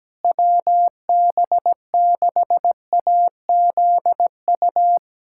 Morse code, heard as a keyed tone, about 17 wpm, 700 Hz. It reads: WB6AZU
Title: USA Callsign DE Callsign - 6 Characters - 17wpm